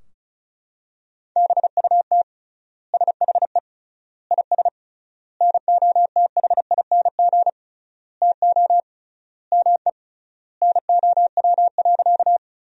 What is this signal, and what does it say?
Morse code, heard as a keyed tone, about 34 wpm, 700 Hz. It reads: BUT SHE IS NOTHING TO ME NOW.
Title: 3x Word-Spacing - Sentences from Top 500 Words - 35wpm